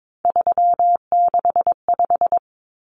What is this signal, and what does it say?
Morse code, heard as a keyed tone, about 22 wpm, 700 Hz. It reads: 365
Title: Common Cut Numbers - 3-Digit 22wpm